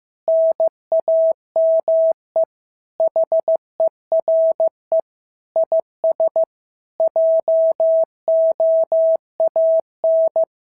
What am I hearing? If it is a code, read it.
Morse code, heard as a keyed tone, about 14 wpm, 650 Hz. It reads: NAME HERE IS JOAN
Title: SOTA QSO Practice 15wpm